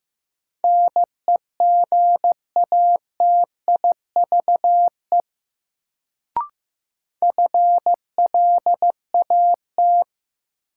Morse code, heard as a keyed tone, about 16 wpm, 700 Hz. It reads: NEGATIVE FLAT
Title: Sentences from Top 1000 Words - New Word Drill - 15wpm